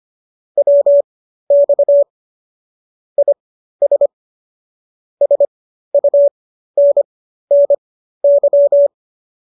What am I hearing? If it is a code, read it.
Morse code, heard as a keyed tone, about 25 wpm, spaced as for 13 wpm, 570 Hz. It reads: WX IS SUNNY